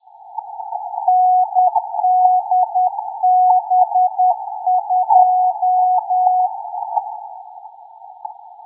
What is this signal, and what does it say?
Morse code, heard as a keyed tone, about 10 wpm, 730 Hz. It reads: NDB2
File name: pix / NDB2-Jun3